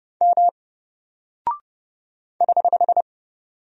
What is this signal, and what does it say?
Morse code, heard as a keyed tone, about 30 wpm, 700 Hz. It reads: M <HH>
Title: Single Character (Letter:Number:Punctuation:Prosign) Alt 30wpm